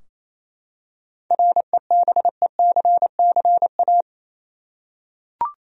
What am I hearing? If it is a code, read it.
Morse code, heard as a keyed tone, about 27 wpm, 700 Hz. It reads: REBECCA
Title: USA Common Names - 28wpm